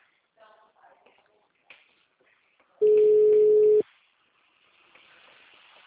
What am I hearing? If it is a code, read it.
Ringback tone (Europe, 425 Hz).